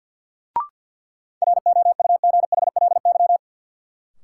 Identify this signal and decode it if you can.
Morse code, heard as a keyed tone, about 53 wpm, 700 Hz. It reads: ROUGHLY
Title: Sentences from Top 1500 Words - New Word Drill - 50wpm